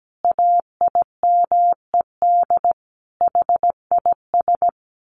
Morse code, heard as a keyed tone, about 17 wpm, 700 Hz. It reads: AIMED HIS
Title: A Sound Of Thunder 17wpm